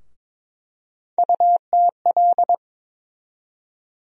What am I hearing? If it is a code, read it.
Morse code, heard as a keyed tone, about 22 wpm, 700 Hz. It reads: UTL
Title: USA Call Sign Postfixes - 3 Letters 22wpm